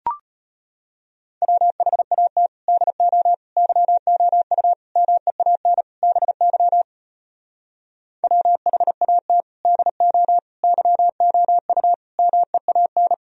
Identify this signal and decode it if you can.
Morse code, heard as a keyed tone, about 33 wpm, 700 Hz. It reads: WHAT DO YOU MEAN BY WHAT DO YOU MEAN